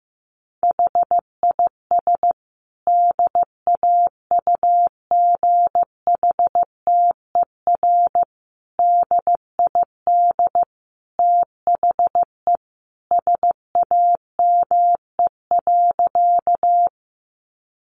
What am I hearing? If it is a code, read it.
Morse code, heard as a keyed tone, about 15 wpm, 700 Hz. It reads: HIS DAUGHTER DID THE SAME.
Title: Sentences from Top 1500 Words 15wpm